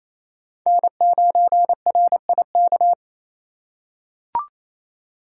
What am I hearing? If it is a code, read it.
Morse code, heard as a keyed tone, about 28 wpm, 700 Hz. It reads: N9RIK
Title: USA Call Signs - 5 Characters - 2x Character Spacing - 28wpm